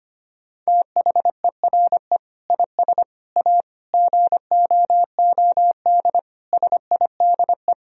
Morse code, heard as a keyed tone, about 25 wpm, 700 Hz. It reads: THERE IS A GOOD SIDE
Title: Sentences from Top 1000 Words - Review Part 1 - 25wpm